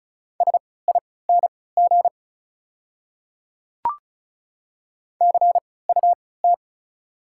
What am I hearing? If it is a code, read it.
Morse code, heard as a keyed tone, about 34 wpm, spaced as for 20 wpm, 700 Hz. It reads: SING CUT
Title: 3x Character Spacing - Top 500 Words - 35wpm